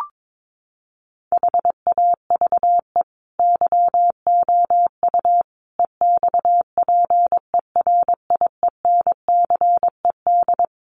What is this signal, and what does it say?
Morse code, heard as a keyed tone, about 22 wpm, 700 Hz. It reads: HAVE YOU EXPERIENCED
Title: Sentences from Top 1100 Words 22wpm